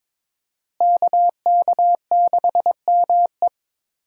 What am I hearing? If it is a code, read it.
Morse code, heard as a keyed tone, about 22 wpm, 700 Hz. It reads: KK6ME
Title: USA Call Signs - 5 Characters - Repeated 2x - 22wpm